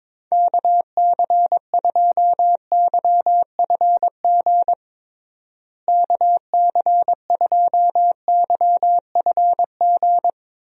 Morse code, heard as a keyed tone, about 22 wpm, 700 Hz. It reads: KC2YFG KC2YFG
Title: USA Call Signs - 6 Characters - Repeated 2x - 22wpm